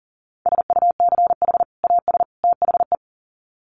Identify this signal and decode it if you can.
Morse code, heard as a keyed tone, about 40 wpm, 700 Hz. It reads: SUCH AS THE
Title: Sets of 3 Words 40wpm